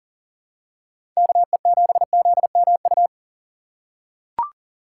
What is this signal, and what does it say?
Morse code, heard as a keyed tone, about 40 wpm, 700 Hz. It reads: KE7ZMU
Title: USA Call Signs - 6 Characters - 4x Character Spacing - 40wpm